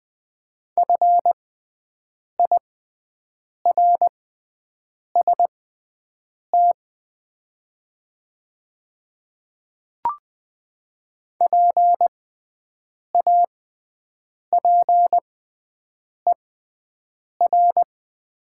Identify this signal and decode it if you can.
Morse code, heard as a keyed tone, about 20 wpm, spaced as for 7 wpm, 700 Hz. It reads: FIRST PAPER
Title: 6x Character Spacing - Top 300 Words - 20wpm